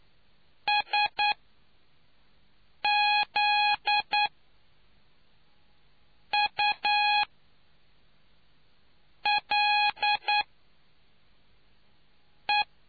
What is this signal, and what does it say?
Morse code, heard as a keyed tone, about 9 wpm, spaced as for 4 wpm, 830 Hz. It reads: SZULE